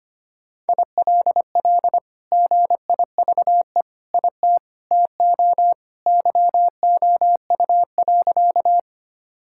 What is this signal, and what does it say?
Morse code, heard as a keyed tone, about 25 wpm, 700 Hz. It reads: ILL GIVE IT TO YOU.
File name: practice / 25wpm / Sentences with 2-Word Phrases - 25wpm